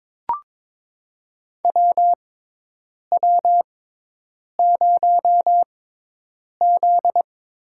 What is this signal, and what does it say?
Morse code, heard as a keyed tone, about 22 wpm, spaced as for 8 wpm, 700 Hz. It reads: WW0Z
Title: USA Call Signs - 4 Characters - 6x Character Spacing - 22wpm